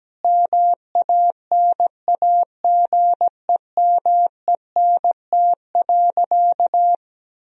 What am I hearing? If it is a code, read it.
Morse code, heard as a keyed tone, about 17 wpm, 700 Hz. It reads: MANAGEMENT.
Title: Sentences from Top 1000 Words - Review Part 4 - 17wpm